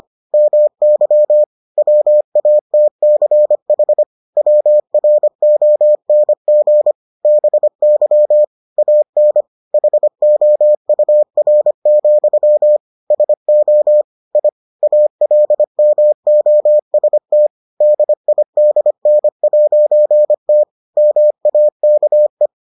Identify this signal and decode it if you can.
Morse code, heard as a keyed tone, about 25 wpm, 600 Hz. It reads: MY WATCH WRONG BY AN HOUR, SO I ALMOST DIDN'T MAKE